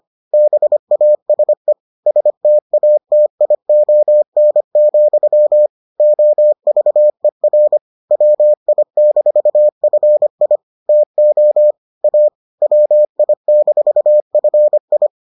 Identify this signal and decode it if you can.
Morse code, heard as a keyed tone, about 25 wpm, 600 Hz. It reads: BASE STATION, OVER WI-FI TO A WI-FI